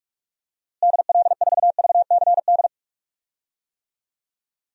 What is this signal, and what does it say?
Morse code, heard as a keyed tone, about 44 wpm, 700 Hz. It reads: DL4VCD
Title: DXCC-100 Call Signs - 6 Characters - Repeated 2x - 45wpm